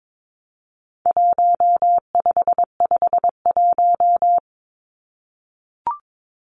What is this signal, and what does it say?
Morse code, heard as a keyed tone, about 22 wpm, 700 Hz. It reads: 1551
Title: Numbers - Contest Style - 4-Digit 22wpm